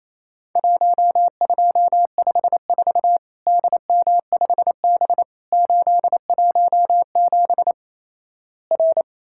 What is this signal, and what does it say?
Morse code, heard as a keyed tone, about 28 wpm, 700 Hz. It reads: 1254 DM5B 817 R
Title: Worked All Europe DX Contest - QTC Exchange - 28wpm